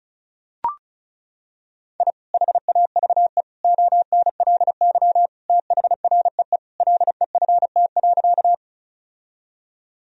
Morse code, heard as a keyed tone, about 36 wpm, 700 Hz. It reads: I HAVE ONLY THREE LEFT.
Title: Sentences from Top 1000 Words - Review Part 1 - 35wpm